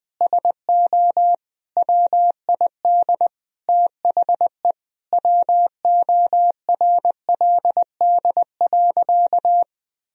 Morse code, heard as a keyed tone, about 20 wpm, 700 Hz. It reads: SO WID THE WORLD.
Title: Sentences from Top 200 Words and Abbreviations 20wpm